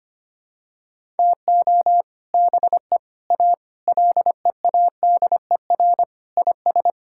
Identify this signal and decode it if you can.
Morse code, heard as a keyed tone, about 25 wpm, 700 Hz. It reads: TO BE A LEADER IS
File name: practice / 25wpm / Sentences with 3-Word Phrases - 25wpm